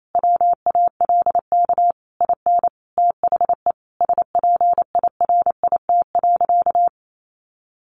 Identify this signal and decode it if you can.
Morse code, heard as a keyed tone, about 28 wpm, 700 Hz. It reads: WALK IN THE SPIRIT.
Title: Sentences from Top 1400 Words 28wpm